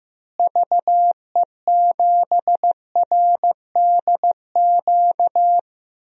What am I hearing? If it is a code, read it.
Morse code, heard as a keyed tone, about 15 wpm, 700 Hz. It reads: VE7RDQ
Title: DXCC-100 Call Signs - 6 Characters - Repeated 3x - 15wpm